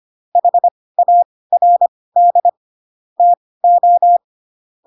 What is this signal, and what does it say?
Morse code, heard as a keyed tone, about 25 wpm, spaced as for 18 wpm, 700 Hz. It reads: HARD TO
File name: Pinocchio ch3 12 25 ik1hge